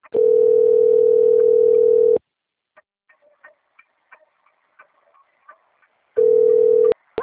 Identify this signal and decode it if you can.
Ringback tone (North America), 440 + 480 Hz.